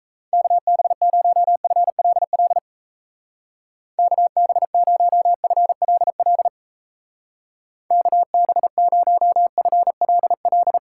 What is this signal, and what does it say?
Morse code, heard as a keyed tone, about 35 wpm, 700 Hz. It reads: KB0FLL KB0FLL KB0FLL
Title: Speed Racing - USA 2x3 Call Signs 28wpm